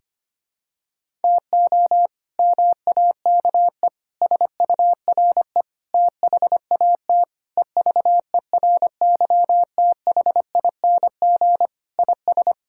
Morse code, heard as a keyed tone, about 25 wpm, 700 Hz. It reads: TO MAKE SURE THAT EVERYTHING IS
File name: practice / 25wpm / Sentences with 4-Word Phrases - 25wpm